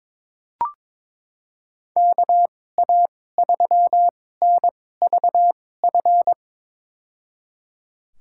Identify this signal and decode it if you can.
Morse code, heard as a keyed tone, about 22 wpm, spaced as for 16 wpm, 700 Hz. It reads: KA3NVF
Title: USA Call Signs - 6 Characters - 2x Character Spacing - 22wpm